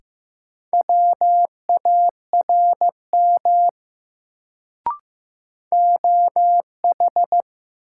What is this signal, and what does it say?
Morse code, heard as a keyed tone, about 15 wpm, 700 Hz. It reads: WARM OH
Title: Sentences from Top 500 Words - New Word Drill - 15wpm